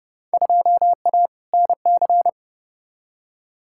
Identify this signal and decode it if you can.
Morse code, heard as a keyed tone, about 32 wpm, 700 Hz. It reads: 2A NC
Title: Field Day Exchanges - 30wpm